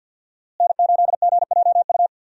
Morse code, heard as a keyed tone, about 46 wpm, 700 Hz. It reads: N8GJU
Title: USA Call Signs - 5 Characters - Repeated 2x - 50wpm